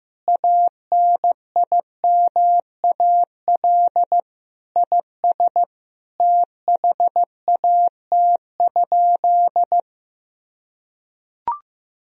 Morse code, heard as a keyed tone, about 15 wpm, 700 Hz. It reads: ANIMAL IS THAT?
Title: Sentences from Top 1000 Words - Review Part 1 - 15wpm